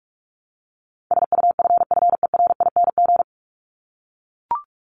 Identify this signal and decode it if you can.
Morse code, heard as a keyed tone, about 44 wpm, 700 Hz. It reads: SUFFERING
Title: ING Suffix - Top 100 Words with Base Words 45wpm